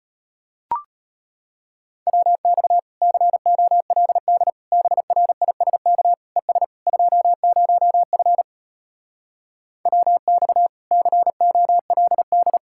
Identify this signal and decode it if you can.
Morse code, heard as a keyed tone, about 33 wpm, 700 Hz. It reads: WX COLD BRISK ES 20F WX COLD